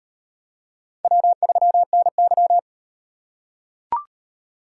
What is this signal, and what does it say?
Morse code, heard as a keyed tone, about 37 wpm, 700 Hz. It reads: W3NY